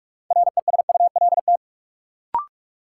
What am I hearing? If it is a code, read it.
Morse code, heard as a keyed tone, about 46 wpm, 700 Hz. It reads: RESULT
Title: Sets of 3 Words - New Word Drill - 45wpm